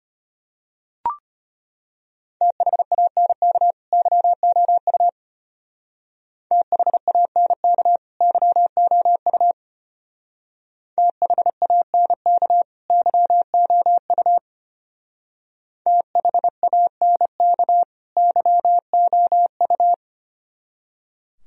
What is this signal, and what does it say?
Morse code, heard as a keyed tone, about 29 wpm, 700 Hz. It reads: THANK YOU THANK YOU THANK YOU THANK YOU
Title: Speed-Racing - Sets of 2 Words 25wpm